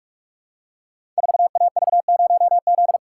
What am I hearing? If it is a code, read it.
Morse code, heard as a keyed tone, about 44 wpm, 690 Hz. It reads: 4AV0Z
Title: Code Groups - 5 Letter:Numbers - 45wpm